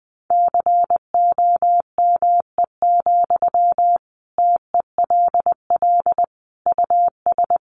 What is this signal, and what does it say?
Morse code, heard as a keyed tone, about 20 wpm, 700 Hz. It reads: COME, TELL US
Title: Sentences from Top 800 Words 20wpm